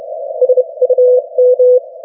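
Morse code, heard as a keyed tone, about 32 wpm, spaced as for 25 wpm, 500 Hz. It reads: SUM
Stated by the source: Morse from the New Zealand Net, the national amateur radio net of Aotearoa/New Zealand, meeting daily on 3535.0 kHz.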